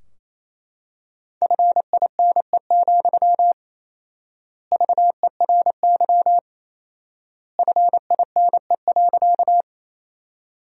Morse code, heard as a keyed tone, about 27 wpm, 700 Hz. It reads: FINE, VERY FINE.
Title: 4x Word-Spacing - Sentences from Top 500 Words - 28wpm